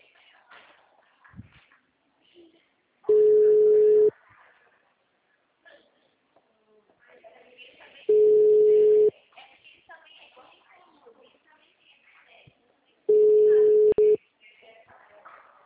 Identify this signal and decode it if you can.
Ringback tone (Europe, 425 Hz), with speech.